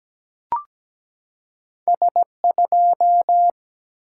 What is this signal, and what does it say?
Morse code, heard as a keyed tone, about 17 wpm, 700 Hz. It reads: S2
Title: Single-Letter Single-Number 17wpm